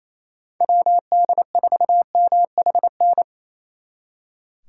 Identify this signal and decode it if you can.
Morse code, heard as a keyed tone, about 35 wpm, 700 Hz. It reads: WD4MHN
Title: Speed Racing - USA 2x3 Call Signs 28wpm